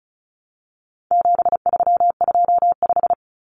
Morse code, heard as a keyed tone, about 34 wpm, 700 Hz. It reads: 7325
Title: Numbers - 4-Digit 35wpm